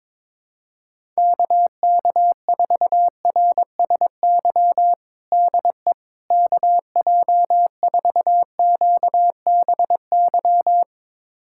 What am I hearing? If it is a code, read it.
Morse code, heard as a keyed tone, about 22 wpm, 700 Hz. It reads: KK4RSY DE KJ4QBY
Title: USA Callsign DE Callsign - 6 Characters - 22wpm